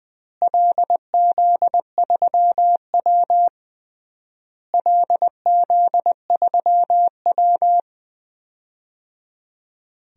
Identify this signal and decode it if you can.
Morse code, heard as a keyed tone, about 21 wpm, 700 Hz. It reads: LZ3W LZ3W
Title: DXCC-100 Call Signs - 4 Characters - Repeated 3x - 20wpm